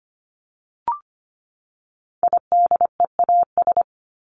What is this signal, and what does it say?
Morse code, heard as a keyed tone, about 25 wpm, 700 Hz. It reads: IDEAS
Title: ES:S Suffix - Top 100 Words with Base Words 25wpm